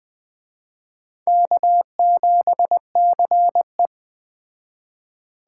Morse code, heard as a keyed tone, about 20 wpm, 700 Hz. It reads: K7CE
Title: Speed Racing - USA 1x2 Call Signs 17wpm